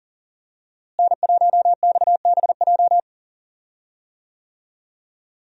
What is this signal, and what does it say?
Morse code, heard as a keyed tone, about 40 wpm, 700 Hz. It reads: N1XBJ
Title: USA Call Signs - 5 Characters - Repeated 2x - 40wpm